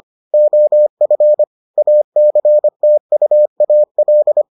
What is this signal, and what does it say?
Morse code, heard as a keyed tone, about 25 wpm, 600 Hz. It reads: OF ACTUAL